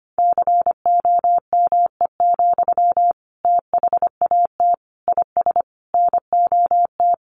Morse code, heard as a keyed tone, about 25 wpm, 700 Hz. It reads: COME, THAT IS NOT
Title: Sentences from Top 700 Words 25wpm